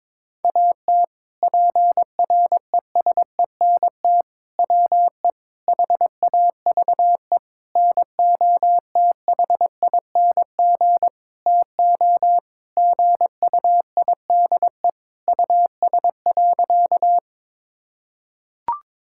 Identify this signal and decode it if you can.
Morse code, heard as a keyed tone, about 22 wpm, 700 Hz. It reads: AT PRESENT WE HAVE NOTHING TO GUIDE US.
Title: Sentences from Top 1000 Words - Review Part 5 - 22wpm